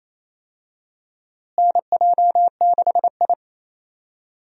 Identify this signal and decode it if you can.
Morse code, heard as a keyed tone, about 28 wpm, 700 Hz. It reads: NJ6I
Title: USA Call Signs - 4 Characters - Repeated 2x - 28wpm